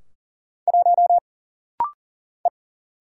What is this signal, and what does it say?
Morse code, heard as a keyed tone, about 40 wpm, 700 Hz. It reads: 1 E
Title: Single Letter-Number - Rapid-Fire 40wpm